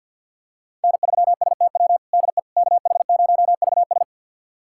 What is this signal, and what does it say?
Morse code, heard as a keyed tone, about 47 wpm, 700 Hz. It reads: N3STW DE KH0VS
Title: USA Callsign DE Callsign - 5 Characters - 50wpm